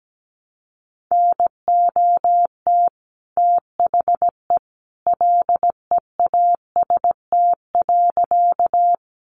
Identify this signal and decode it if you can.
Morse code, heard as a keyed tone, about 17 wpm, 700 Hz. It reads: NOT THE LEAST.
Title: Sentences from Top 700 Words 17wpm